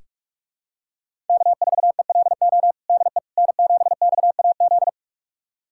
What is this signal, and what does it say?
Morse code, heard as a keyed tone, about 41 wpm, 700 Hz. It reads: K4ELO DE N7XAZ
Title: USA Callsign DE Callsign 45wpm